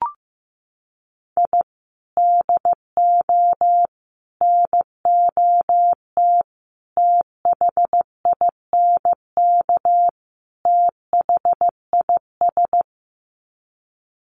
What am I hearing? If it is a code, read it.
Morse code, heard as a keyed tone, about 15 wpm, 700 Hz. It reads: I DO NOT THINK THIS
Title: Sets of 5 Words Encore 15wpm